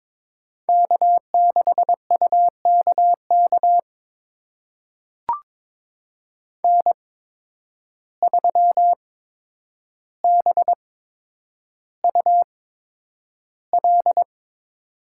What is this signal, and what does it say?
Morse code, heard as a keyed tone, about 22 wpm, spaced as for 9 wpm, 700 Hz. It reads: K6UKK N3BUL
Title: USA Call Signs - 5 Characters - 8x Character Spacing - 22wpm